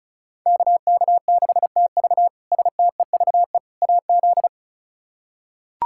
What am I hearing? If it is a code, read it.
Morse code, heard as a keyed tone, about 36 wpm, 700 Hz. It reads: KK6TV STEVE AZ
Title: K1USN SST QSO Practice - 35wpm